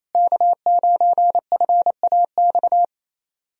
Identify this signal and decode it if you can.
Morse code, heard as a keyed tone, about 28 wpm, 700 Hz. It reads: K9FAX
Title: USA Call Signs - 5 Characters - Repeated 3x - 28wpm